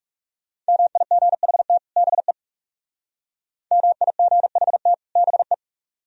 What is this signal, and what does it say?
Morse code, heard as a keyed tone, about 38 wpm, 700 Hz. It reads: MIGHT BE MIGHT BE